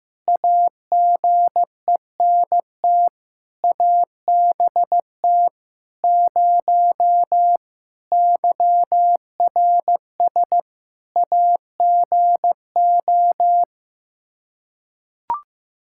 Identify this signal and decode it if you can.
Morse code, heard as a keyed tone, about 16 wpm, 700 Hz. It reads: AGENT ABT 0 YRS AGO
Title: Practice QSO - Exchange 3 - More Info - 15wpm